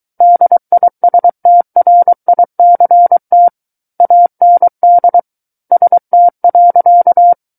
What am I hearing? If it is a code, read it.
Morse code, heard as a keyed tone, about 23 wpm, 700 Hz. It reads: DISTRICT AND ST.
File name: Book One / 14-Chapter 14 - 23 WPM